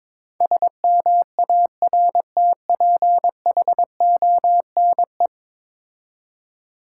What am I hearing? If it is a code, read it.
Morse code, heard as a keyed tone, about 23 wpm, 700 Hz. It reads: SMARTPHONE
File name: practice / 22wpm / 10 Letter Words - First 3 Letters Given - 22wpm